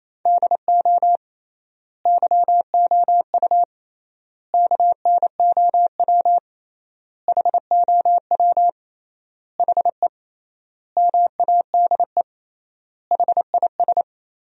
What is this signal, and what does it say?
Morse code, heard as a keyed tone, about 28 wpm, 700 Hz. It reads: DO YOU KNOW HOW HE MADE HIS
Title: 3x Word-Spacing - Sentences from Top 300 Words - 28wpm